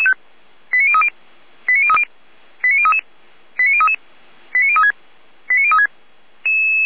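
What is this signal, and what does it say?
Five-tone selective call (ZVEI), heard as quick runs of five tones, tones of about 70 ms. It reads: 89030 89030 89033 89033 89036 89036
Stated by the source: ZVEI five-tone call received by SDR.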